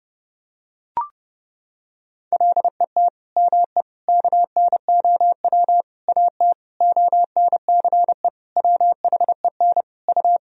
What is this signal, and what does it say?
Morse code, heard as a keyed tone, about 30 wpm, 700 Hz. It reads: LET ME KNOW AT ONCE WHEN U
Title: Sentences from Top 300 Words and Abbreviations 30wpm